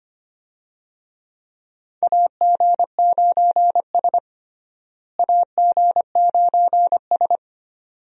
Morse code, heard as a keyed tone, about 25 wpm, 700 Hz. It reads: AG9S AG9S
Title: USA Call Signs - 4 Characters - Repeated 2x - 25wpm